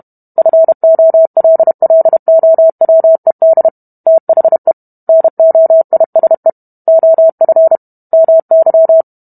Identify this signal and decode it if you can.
Morse code, heard as a keyed tone, about 32 wpm, 650 Hz. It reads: FOLLOWED THE NOISE OF MY